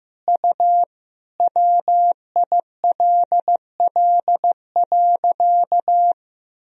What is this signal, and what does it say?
Morse code, heard as a keyed tone, about 15 wpm, 700 Hz. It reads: U WILL.
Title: Sentences from Top 100 Words and Abbreviations 15wpm